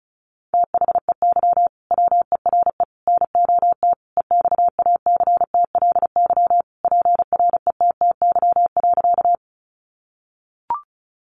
Morse code, heard as a keyed tone, about 34 wpm, 700 Hz. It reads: THEY WERE NOT EXACTLY PRETTY.
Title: Sentences from Top 1000 Words 35wpm